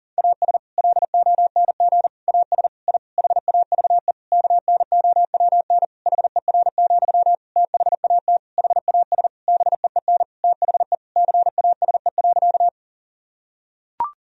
Morse code, heard as a keyed tone, about 37 wpm, 700 Hz. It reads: AS LONG AS I HAVE KNOWN HER, THAT HAS BEEN THE CASE.
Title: Sentences from Top 1000 Words - Review Part 3 - 40wpm